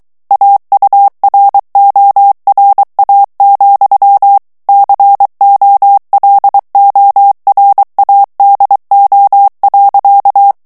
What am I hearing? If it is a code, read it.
Morse code, heard as a keyed tone, about 23 wpm, 800 Hz. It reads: AURORA, COLORADO.
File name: ANGLAIS texte clair 3 vitesse 22 mots minute